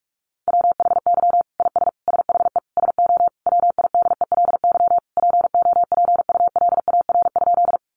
Morse code, heard as a keyed tone, about 44 wpm, 700 Hz. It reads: WHY IS SHE SO WIDELY POPULAR?
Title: Sentences from Top 1200 Words 45wpm